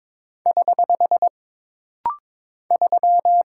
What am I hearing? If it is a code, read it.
Morse code, heard as a keyed tone, about 22 wpm, 700 Hz. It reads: <HH> 3
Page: This rapid-fire timing is geared to push your ICR (Instant Character Recognition) skills.